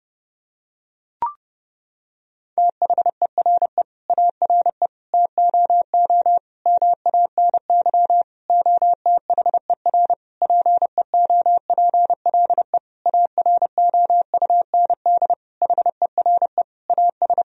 Morse code, heard as a keyed tone, about 31 wpm, 700 Hz. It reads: THERE ARE TOO MANY OTHER PEOPLE AROUND HERE AS
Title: Sentences from Top 1000 Words - Review Part 3 - 30wpm